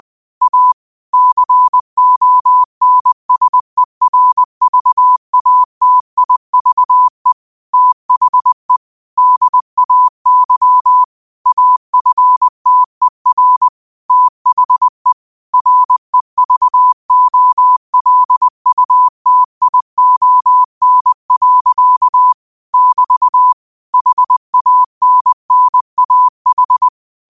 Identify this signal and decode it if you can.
Morse code, heard as a keyed tone, about 20 wpm, 1000 Hz. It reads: A CONSERVATIVE THE DAY AFTER THE REVOLUTION. <BT> HANNAH
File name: qotdfiles / 20wpm